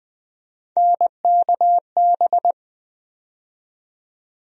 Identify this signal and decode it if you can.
Morse code, heard as a keyed tone, about 21 wpm, 700 Hz. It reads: NKB
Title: USA Call Sign Postfixes - 3 Letters 20wpm